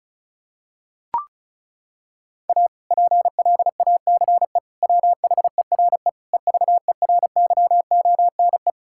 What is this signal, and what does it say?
Morse code, heard as a keyed tone, about 36 wpm, 700 Hz. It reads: A PLACE WHERE EVERYONE
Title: Sentences from Top 1000 Words - Review Part 5 - 35wpm